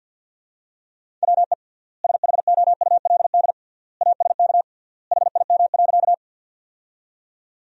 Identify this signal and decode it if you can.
Morse code, heard as a keyed tone, about 51 wpm, 700 Hz. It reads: WE SHOULD ASK HIM.
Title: 3x Word-Spacing - Sentences from Top 300 Words - 50wpm